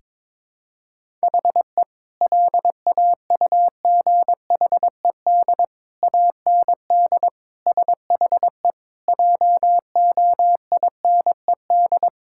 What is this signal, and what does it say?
Morse code, heard as a keyed tone, about 22 wpm, 700 Hz. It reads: HE LAUGHED AND SHE JOINED
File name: practice / 22wpm / Sentences with 2-Word Phrases - 22wpm